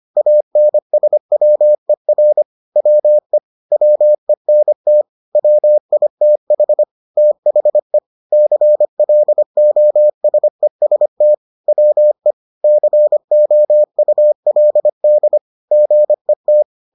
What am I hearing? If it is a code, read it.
Morse code, heard as a keyed tone, about 25 wpm, 600 Hz. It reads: ANSWER WE WENT WITH THE CLOSEST WE COULD GET